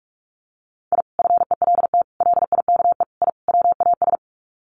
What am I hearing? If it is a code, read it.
Morse code, heard as a keyed tone, about 44 wpm, 700 Hz. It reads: I FELT LIKE I WAS
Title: Sets of 5 Words Encore 45wpm